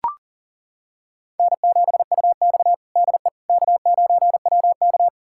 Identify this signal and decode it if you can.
Morse code, heard as a keyed tone, about 40 wpm, 700 Hz. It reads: N7UX DE K9WK
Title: USA Callsign DE Callsign - 4 Characters - 40wpm